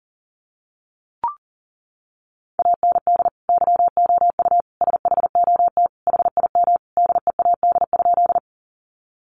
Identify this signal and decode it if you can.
Morse code, heard as a keyed tone, about 40 wpm, 700 Hz. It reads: AND YOU SHOT HIM DEAD?
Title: Sentences from Top 1500 Words 40wpm